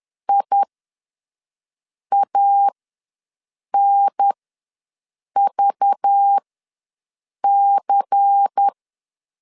Morse code, heard as a keyed tone, about 10 wpm, spaced as for 5 wpm, 800 Hz. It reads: IANVC
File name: LEZIONI / 7 GF9Z / 7 GF9 ws 10wpm 4034